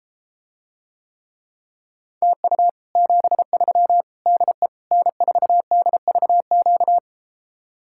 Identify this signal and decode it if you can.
Morse code, heard as a keyed tone, about 33 wpm, 700 Hz. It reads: TU 73 DE N4DVQ